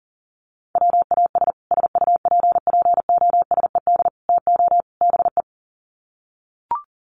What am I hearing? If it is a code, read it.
Morse code, heard as a keyed tone, about 38 wpm, 700 Hz. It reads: WAS SUPPOSED TO BE
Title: Sets of 4 Words 40wpm